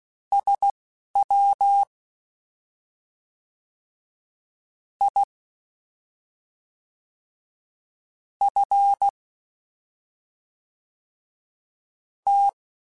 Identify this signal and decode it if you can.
Morse code, heard as a keyed tone, about 16 wpm, spaced as for 3 wpm, 800 Hz. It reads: SWIFT